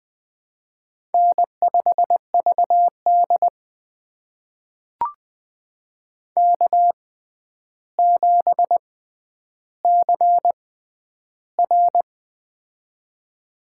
Morse code, heard as a keyed tone, about 20 wpm, spaced as for 10 wpm, 700 Hz. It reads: N5VD K7CR
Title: USA Call Signs - 4 Characters - 6x Character Spacing - 20wpm